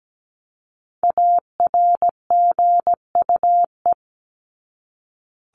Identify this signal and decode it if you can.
Morse code, heard as a keyed tone, about 17 wpm, 700 Hz. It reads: ARGUE
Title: Top 1501-1750 Words 17wpm